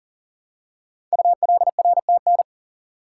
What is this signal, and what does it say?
Morse code, heard as a keyed tone, about 42 wpm, 700 Hz. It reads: ULRTN
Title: Code Groups - 5 Letters - 40wpm